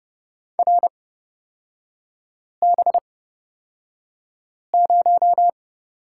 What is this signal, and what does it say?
Morse code, heard as a keyed tone, about 30 wpm, spaced as for 5 wpm, 700 Hz. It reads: RB0
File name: Lesson 36e - BT - Three Characters 30wpm